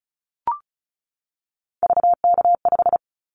Morse code, heard as a keyed tone, about 36 wpm, 700 Hz. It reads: VK5
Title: Callsign Prefixes for Top 50 DXCC - 35wpm